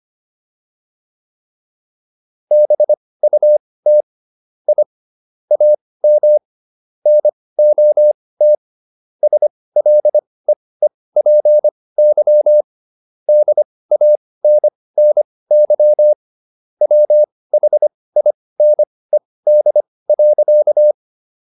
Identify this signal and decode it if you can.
Morse code, heard as a keyed tone, about 25 wpm, spaced as for 18 wpm, 600 Hz. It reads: BUT I AM NOT SLEEPY DANNY WHINED.